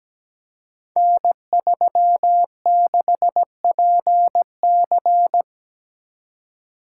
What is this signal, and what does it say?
Morse code, heard as a keyed tone, about 17 wpm, 700 Hz. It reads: N36PC
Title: Code Groups - 5 Letter:Numbers - 17wpm